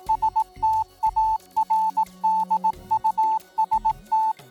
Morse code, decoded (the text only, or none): STARDUST